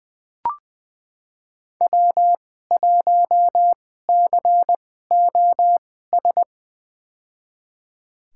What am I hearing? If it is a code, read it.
Morse code, heard as a keyed tone, about 20 wpm, spaced as for 14 wpm, 700 Hz. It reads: W1COS
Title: USA Call Signs - 5 Characters - 2x Character Spacing - 20wpm